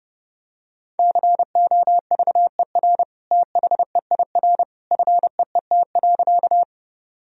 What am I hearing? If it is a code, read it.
Morse code, heard as a keyed tone, about 32 wpm, 700 Hz. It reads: COVER THEIR FEET.
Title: Sentences from Top 1000 Words - Review Part 1 - 30wpm